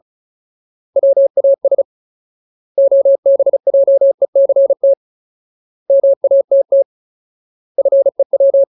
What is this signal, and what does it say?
Morse code, heard as a keyed tone, about 34 wpm, 550 Hz. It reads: WAS OBJECT MATT FEW
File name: Ultimate Word List / 35WPM Ultimate